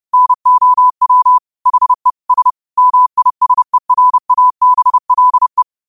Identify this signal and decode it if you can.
Morse code, heard as a keyed tone, about 30 wpm, 1000 Hz. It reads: NOW HE S MISERABLE
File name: qotdfiles / 30wpm